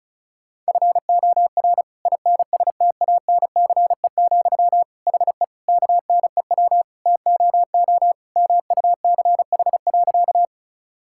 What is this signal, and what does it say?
Morse code, heard as a keyed tone, about 36 wpm, 700 Hz. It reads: FOR INSTANCE, HE KNEW TOO MUCH.
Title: Sentences from Top 1000 Words - Review Part 5 - 35wpm